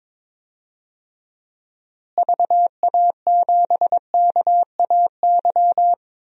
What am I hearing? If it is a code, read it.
Morse code, heard as a keyed tone, about 22 wpm, 700 Hz. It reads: VA7KAY
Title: DXCC-100 Call Signs - 6 Characters - Repeated 2x - 22wpm